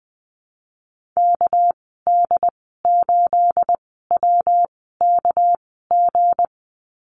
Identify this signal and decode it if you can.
Morse code, heard as a keyed tone, about 20 wpm, spaced as for 14 wpm, 700 Hz. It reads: KD8WKG
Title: 245 Lesson 39g - Letter z - Call Signs 20wpm